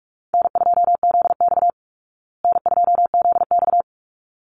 Morse code, heard as a keyed tone, about 44 wpm, 700 Hz. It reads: N2ZX N2ZX
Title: Call Signs Repeated 3x - 45wpm